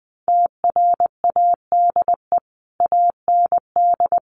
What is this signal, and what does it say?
Morse code, heard as a keyed tone, about 20 wpm, 700 Hz. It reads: TRADE AND
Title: Sentences from Top 1500 Words 20wpm